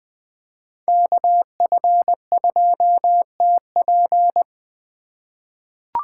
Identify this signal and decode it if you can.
Morse code, heard as a keyed tone, about 21 wpm, 700 Hz. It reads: KF2TP
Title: K1USN SST QSO Practice - 20wpm